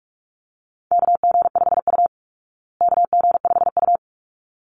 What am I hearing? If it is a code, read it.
Morse code, heard as a keyed tone, about 44 wpm, 700 Hz. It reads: KG5U KG5U
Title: Call Signs Repeated 3x - 45wpm